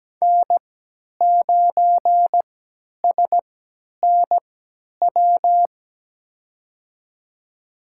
Morse code, heard as a keyed tone, about 17 wpm, spaced as for 10 wpm, 700 Hz. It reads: N9SNW
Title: USA Call Signs - 5 Characters - 3x Character Spacing - 17wpm